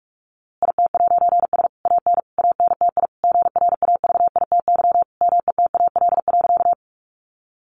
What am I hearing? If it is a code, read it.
Morse code, heard as a keyed tone, about 44 wpm, 700 Hz. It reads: IT'S AN ANTI GRAVITY METAL.
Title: A Sound Of Thunder 45wpm